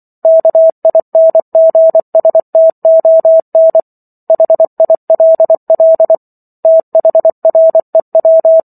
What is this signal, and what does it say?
Morse code, heard as a keyed tone, about 24 wpm, 650 Hz. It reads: KINGSTON HILL THREW